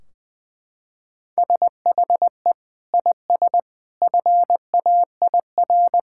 Morse code, heard as a keyed tone, about 21 wpm, 700 Hz. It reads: SHE IS FAIR
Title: Sentences from Top 1600 Words 20wpm